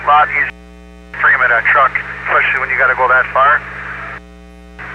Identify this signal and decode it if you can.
Emergency Alert System (SAME) tones, heard as harsh buzzing bursts, with speech. It is heard from 1.1 to 4.0 s.